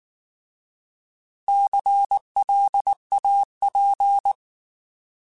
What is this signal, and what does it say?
Morse code, heard as a keyed tone, about 19 wpm, 800 Hz. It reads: CLAP
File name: Alphabetical / Alphabetical mp3s / Cs @18